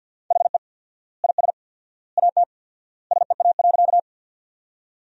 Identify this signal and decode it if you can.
Morse code, heard as a keyed tone, about 50 wpm, 700 Hz. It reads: HE IS AT SEA.
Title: 4x Word-Spacing - Sentences from Top 300 Words - 50wpm